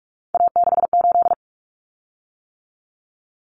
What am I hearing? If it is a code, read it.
Morse code, heard as a keyed tone, about 44 wpm, 700 Hz. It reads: A68
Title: Common Cut Numbers - 3-Digit 45wpm